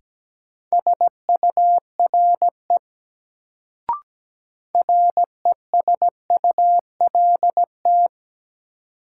Morse code, heard as a keyed tone, about 17 wpm, 700 Hz. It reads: SURE RESULT
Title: Sets of 3 Words - New Word Drill - 17wpm